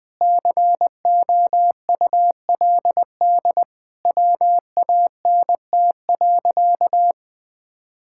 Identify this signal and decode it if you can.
Morse code, heard as a keyed tone, about 21 wpm, 700 Hz. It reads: COULD WANT.
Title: Sentences from Top 1000 Words - Review Part 5 - 20wpm